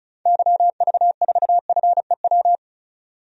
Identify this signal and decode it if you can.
Morse code, heard as a keyed tone, about 36 wpm, 700 Hz. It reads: YV4FEW
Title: DXCC-100 Call Signs - 6 Characters - 35wpm